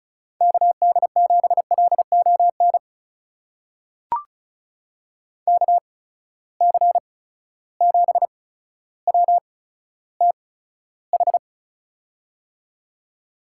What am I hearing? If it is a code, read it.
Morse code, heard as a keyed tone, about 36 wpm, spaced as for 15 wpm, 700 Hz. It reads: KD7LON KC7WTH
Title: USA Call Signs - 6 Characters - 8x Character Spacing - 35wpm